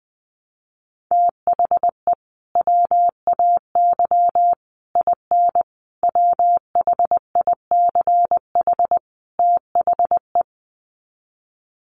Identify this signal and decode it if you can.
Morse code, heard as a keyed tone, about 20 wpm, 700 Hz. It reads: THE WAY IN WHICH THE
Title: Sets of 5 Words Encore2 20wpm